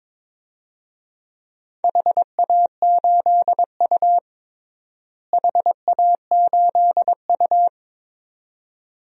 Morse code, heard as a keyed tone, about 21 wpm, 700 Hz. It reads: HA8U HA8U
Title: DXCC-100 Call Signs - 4 Characters - Repeated 2x - 22wpm